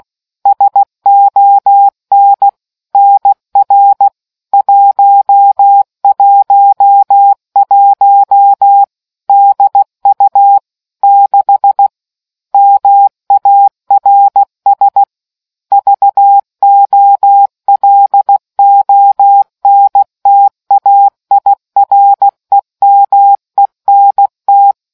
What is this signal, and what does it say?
Morse code, heard as a keyed tone, about 16 wpm, 800 Hz. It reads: SON NR 111 DU 6 MARS VOLONTAIREMENT